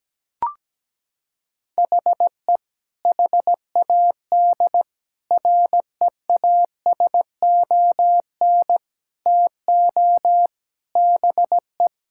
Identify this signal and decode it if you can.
Morse code, heard as a keyed tone, about 17 wpm, 700 Hz. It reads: HE HAD REASON TO BE
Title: Sentences from Top 1000 Words - Review Part 6 - 17wpm